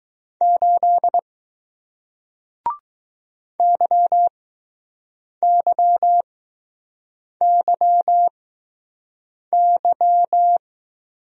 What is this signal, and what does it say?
Morse code, heard as a keyed tone, about 20 wpm, spaced as for 6 wpm, 700 Hz. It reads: 8 YYYY